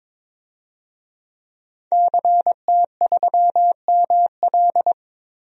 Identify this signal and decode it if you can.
Morse code, heard as a keyed tone, about 21 wpm, 700 Hz. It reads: CT3ML
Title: DXCC-100 Call Signs - 5 Characters - Repeated 3x - 22wpm